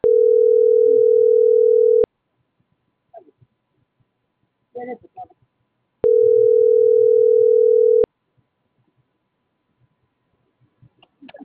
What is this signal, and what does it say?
Ringback tone (North America), 440 + 480 Hz.